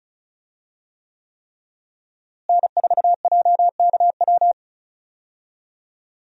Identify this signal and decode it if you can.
Morse code, heard as a keyed tone, about 36 wpm, 700 Hz. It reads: N4JKW
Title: USA Call Signs - 5 Characters - Repeated 3x - 35wpm